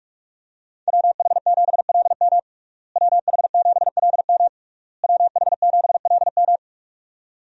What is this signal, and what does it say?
Morse code, heard as a keyed tone, about 46 wpm, 700 Hz. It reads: WH7LM WH7LM WH7LM
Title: USA Call Signs - 5 Characters - Repeated 3x - 45wpm